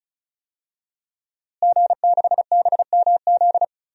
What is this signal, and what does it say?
Morse code, heard as a keyed tone, about 36 wpm, 700 Hz. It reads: G6BMZ
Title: DXCC-100 Call Signs - 5 Characters - Repeated 3x - 35wpm